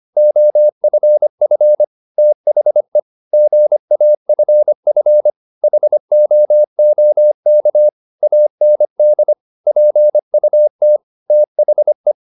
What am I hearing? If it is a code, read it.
Morse code, heard as a keyed tone, about 25 wpm, 600 Hz. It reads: OFF THE GAFF HOOK AND PUT THE
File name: the old man and the sea part 1 25wpm